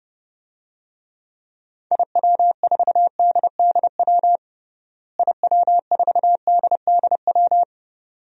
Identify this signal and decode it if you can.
Morse code, heard as a keyed tone, about 31 wpm, 700 Hz. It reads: IW4DDW IW4DDW
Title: DXCC-100 Call Signs - 6 Characters - Repeated 2x - 30wpm